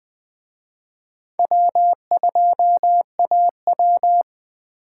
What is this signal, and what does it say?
Morse code, heard as a keyed tone, about 20 wpm, 700 Hz. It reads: W2AW